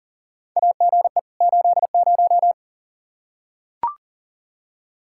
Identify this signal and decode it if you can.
Morse code, heard as a keyed tone, about 40 wpm, 700 Hz. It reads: AGE 80
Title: Practice QSO - Exchange 3 - More Info - 40wpm